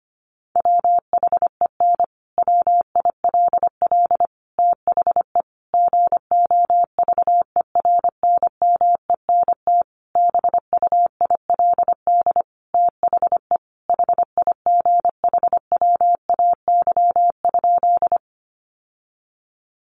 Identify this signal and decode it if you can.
Morse code, heard as a keyed tone, about 25 wpm, 700 Hz. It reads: WHEN WILL THE GOVERNMENT BUILD THE HIGHWAY?
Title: Sentences from Top 1400 Words 25wpm